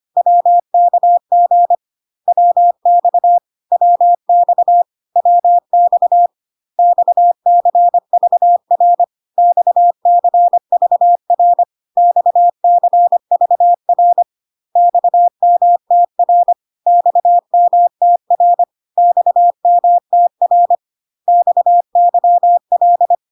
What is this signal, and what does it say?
Morse code, heard as a keyed tone, about 25 wpm, 700 Hz. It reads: WKG WX WX WX XCVR XCVR XCVR XMTR XMTR XMTR XYL
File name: cw / CW common abbreviations alphabetical repeated 3 times 25 25wpm ik1hge